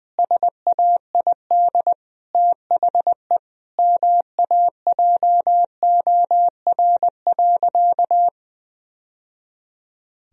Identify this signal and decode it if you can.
Morse code, heard as a keyed tone, about 21 wpm, 700 Hz. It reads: SAID THE MAJOR.
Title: Sentences from Top 1000 Words - Review Part 4 - 20wpm